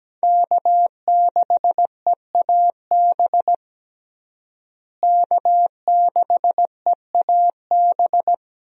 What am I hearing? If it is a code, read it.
Morse code, heard as a keyed tone, about 17 wpm, 700 Hz. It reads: K6EAB K6EAB
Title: USA Call Signs - 5 Characters - Repeated 2x - 17wpm